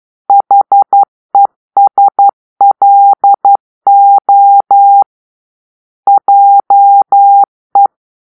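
Morse code, heard as a keyed tone, about 11 wpm, 800 Hz. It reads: HESLO JE
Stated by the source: Morse-code cipher made by a player team for the Kachna puzzle game.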